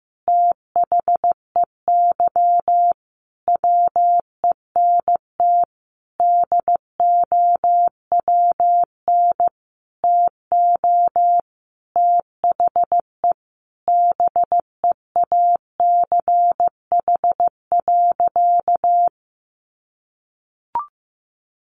Morse code, heard as a keyed tone, about 15 wpm, 700 Hz. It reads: THEY WENT DOWN TO THE BEACH.
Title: Sentences from Top 1500 Words 15wpm